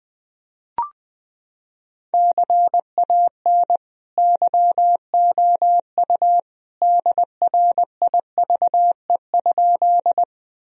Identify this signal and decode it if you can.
Morse code, heard as a keyed tone, about 20 wpm, 700 Hz. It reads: CAN YOU DRIVE?
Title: Sentences from Top 1000 Words - Review Part 2 - 20wpm